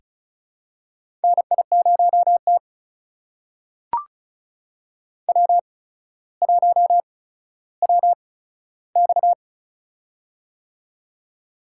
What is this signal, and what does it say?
Morse code, heard as a keyed tone, about 36 wpm, spaced as for 15 wpm, 700 Hz. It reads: NI0T W1WX
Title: USA Call Signs - 4 Characters - 8x Character Spacing - 35wpm